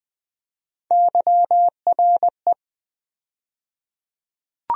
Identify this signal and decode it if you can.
Morse code, heard as a keyed tone, about 21 wpm, 700 Hz. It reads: YRE
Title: USA Call Sign Postfixes - 3 Letters 20wpm